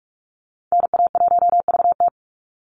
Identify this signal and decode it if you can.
Morse code, heard as a keyed tone, about 44 wpm, 700 Hz. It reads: NA1VT
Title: Call Signs Repeated 2x - 45wpm